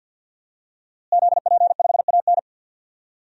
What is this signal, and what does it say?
Morse code, heard as a keyed tone, about 52 wpm, 700 Hz. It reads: ZP5AN
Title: DXCC-100 Call Signs - 5 Characters - Repeated 3x - 50wpm